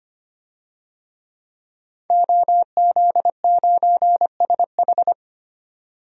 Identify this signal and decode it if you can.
Morse code, heard as a keyed tone, about 25 wpm, 700 Hz. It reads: OZ9SH
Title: DXCC-100 Call Signs - 5 Characters - 25wpm